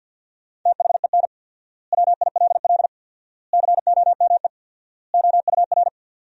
Morse code, heard as a keyed tone, about 50 wpm, 700 Hz. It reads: THEN WILL COME OUR
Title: 4x Word-Spacing - Sentences from Top 200 Words - 50wpm